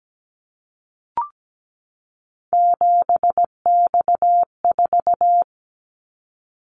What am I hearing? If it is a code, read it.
Morse code, heard as a keyed tone, about 17 wpm, 700 Hz. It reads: 7X4
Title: Callsign Prefixes for Top 100 DXCC - 17wpm